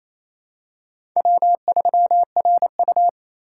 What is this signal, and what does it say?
Morse code, heard as a keyed tone, about 28 wpm, 700 Hz. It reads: W3RU
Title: USA Call Signs - 4 Characters - 4x Character Spacing - 28wpm